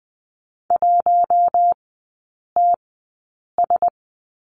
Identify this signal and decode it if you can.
Morse code, heard as a keyed tone, about 20 wpm, spaced as for 8 wpm, 700 Hz. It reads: 1TS
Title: 053 Lesson 10e - Letter r - Three Characters 20wpm